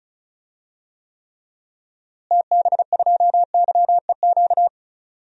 Morse code, heard as a keyed tone, about 34 wpm, 700 Hz. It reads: TB2YEQ
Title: DXCC-100 Call Signs - 6 Characters - Repeated 2x - 35wpm